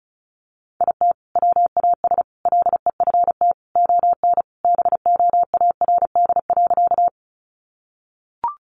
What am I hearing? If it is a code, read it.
Morse code, heard as a keyed tone, about 34 wpm, 700 Hz. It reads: IT WAS LEFT ON BOARD.
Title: Sentences from Top 700 Words 35wpm